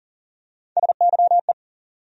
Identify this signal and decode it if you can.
Morse code, heard as a keyed tone, about 40 wpm, 700 Hz. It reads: SYE
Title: USA Call Sign Postfixes - 3 Letters 40wpm